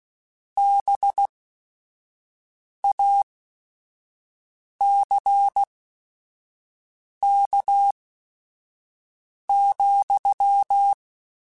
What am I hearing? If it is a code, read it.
Morse code, heard as a keyed tone, about 16 wpm, spaced as for 5 wpm, 800 Hz. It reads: BACK,